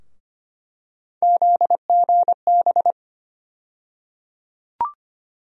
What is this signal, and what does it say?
Morse code, heard as a keyed tone, about 25 wpm, 700 Hz. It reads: ZGB
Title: USA Call Sign Postfixes - 3 Letters 25wpm